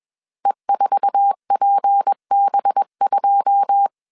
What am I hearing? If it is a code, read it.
Morse code, heard as a keyed tone, about 21 wpm, 800 Hz. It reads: E4PB2
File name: CW / DET / 1 9 1500C S B 20wpm 2313